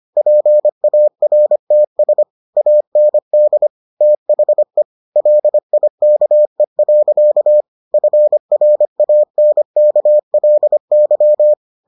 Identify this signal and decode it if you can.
Morse code, heard as a keyed tone, about 25 wpm, 600 Hz. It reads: PARTS AND THE LIKE. FRANKLY